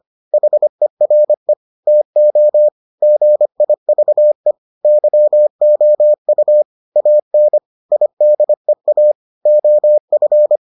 Morse code, heard as a keyed tone, about 25 wpm, 600 Hz. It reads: HERE TO GIVE YOU AN IDEA OF